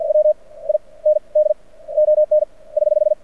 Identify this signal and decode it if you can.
Morse code, heard as a keyed tone, about 48 wpm, spaced as for 27 wpm, 610 Hz. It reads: WIND ON <HH>
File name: archived / CRSnet